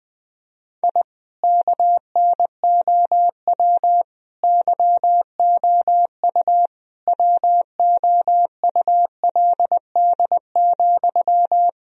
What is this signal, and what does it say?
Morse code, heard as a keyed tone, about 20 wpm, 700 Hz. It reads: I KNOW YOU WOULD,
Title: Sentences from Top 1000 Words - Review Part 1 - 20wpm